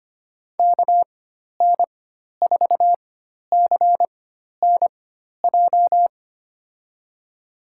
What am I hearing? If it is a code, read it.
Morse code, heard as a keyed tone, about 25 wpm, spaced as for 12 wpm, 700 Hz. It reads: KN4CNJ
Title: USA Call Signs - 6 Characters - 4x Character Spacing - 25wpm